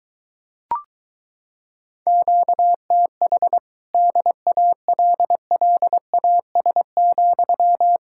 Morse code, heard as a keyed tone, about 23 wpm, 700 Hz. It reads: QTH DALLAS,